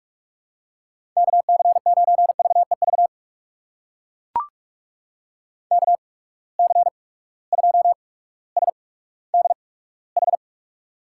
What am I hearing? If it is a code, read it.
Morse code, heard as a keyed tone, about 41 wpm, spaced as for 19 wpm, 700 Hz. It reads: KC9VEV KC2SDH
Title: USA Call Signs - 6 Characters - 8x Character Spacing - 45wpm